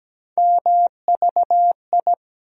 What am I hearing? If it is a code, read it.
Morse code, heard as a keyed tone, about 17 wpm, 700 Hz. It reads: MVI
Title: USA Call Sign Postfixes - 3 Letters 17wpm